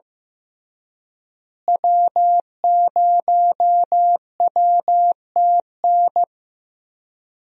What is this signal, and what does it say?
Morse code, heard as a keyed tone, about 15 wpm, 700 Hz. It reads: W0WTN
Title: USA Call Signs - 5 Characters - Repeated 3x - 15wpm